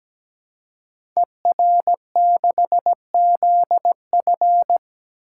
Morse code, heard as a keyed tone, about 17 wpm, 690 Hz. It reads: ER6ZF
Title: Code Groups - 5 Letter:Numbers - 17wpm